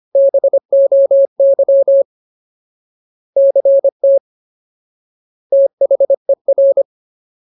Morse code, heard as a keyed tone, about 25 wpm, 550 Hz. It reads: BOY CT THER
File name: Ultimate Word List / 25WPM Ultimate